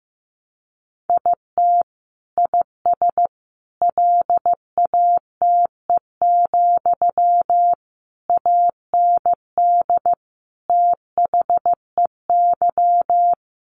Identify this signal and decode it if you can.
Morse code, heard as a keyed tone, about 15 wpm, 700 Hz. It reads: IT IS LATE, AND THEY
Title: Sentences from Top 600 Words 15wpm